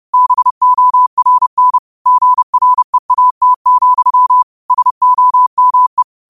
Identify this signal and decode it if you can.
Morse code, heard as a keyed tone, about 30 wpm, 1000 Hz. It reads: BORN GREAT, SOME